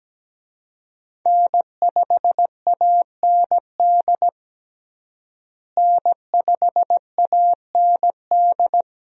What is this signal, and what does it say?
Morse code, heard as a keyed tone, about 17 wpm, 700 Hz. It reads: N5AND N5AND
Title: USA Call Signs - 5 Characters - Repeated 2x - 17wpm